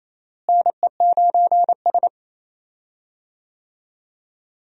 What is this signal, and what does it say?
Morse code, heard as a keyed tone, about 28 wpm, 700 Hz. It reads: NE9S
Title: USA Call Signs - 4 Characters - Repeated 2x - 28wpm